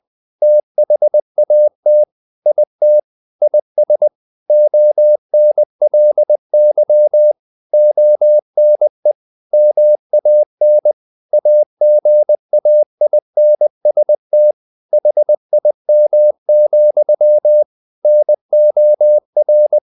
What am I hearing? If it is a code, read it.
Morse code, heard as a keyed tone, about 20 wpm, 600 Hz. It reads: THAT IT IS ONLY ONE MAN AGAINST HIM, NOR
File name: the old man and the sea / the old man and the sea part 1 20wpm